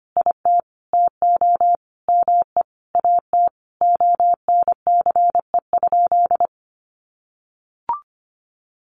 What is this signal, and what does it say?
Morse code, heard as a keyed tone, about 25 wpm, 700 Hz. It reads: IT TO ME AT ONCE?
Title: Sentences from Top 1100 Words 25wpm